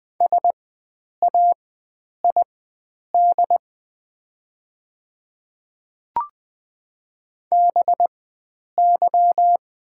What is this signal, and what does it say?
Morse code, heard as a keyed tone, about 20 wpm, spaced as for 9 wpm, 690 Hz. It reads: SAID BY
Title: 4x Character Spacing - Top 100 Words - 20wpm